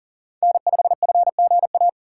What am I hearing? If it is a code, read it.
Morse code, heard as a keyed tone, about 39 wpm, 700 Hz. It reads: N5FGA
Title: USA Call Signs - 2x Character Spacing - 40wpm